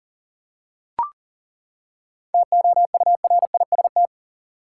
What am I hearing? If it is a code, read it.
Morse code, heard as a keyed tone, about 37 wpm, 700 Hz. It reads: TOURIST
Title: Seven Letter Words - 40wpm